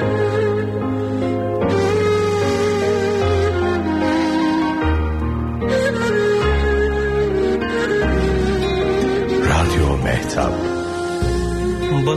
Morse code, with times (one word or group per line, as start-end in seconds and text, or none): none